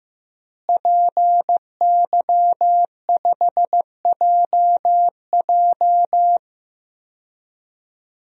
Morse code, heard as a keyed tone, about 15 wpm, 700 Hz. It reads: PY5JJ
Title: DXCC-100 Call Signs - 5 Characters - Repeated 2x - 15wpm